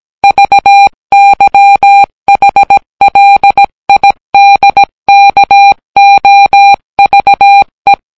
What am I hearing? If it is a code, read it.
Morse code, heard as a keyed tone, about 17 wpm, 800 Hz. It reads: VYHLIDKOVE